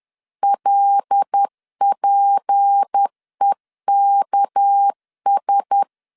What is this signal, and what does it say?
Morse code, heard as a keyed tone, about 10 wpm, 800 Hz. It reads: LPEKS
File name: LEZIONI / 6 PKX0 / 6 PK 10wpm 2436